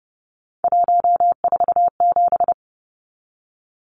Morse code, heard as a keyed tone, about 30 wpm, 700 Hz. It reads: 147
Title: Numbers - 3-Digit 30wpm